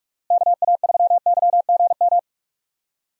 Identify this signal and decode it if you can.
Morse code, heard as a keyed tone, about 46 wpm, 700 Hz. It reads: KA3YGM
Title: USA Call Signs - 6 Characters - Repeated 3x - 45wpm